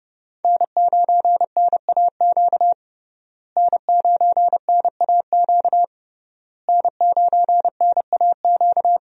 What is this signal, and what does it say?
Morse code, heard as a keyed tone, about 30 wpm, 700 Hz. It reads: N9NAQ N9NAQ N9NAQ
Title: USA Call Signs - 5 Characters - Repeated 3x - 30wpm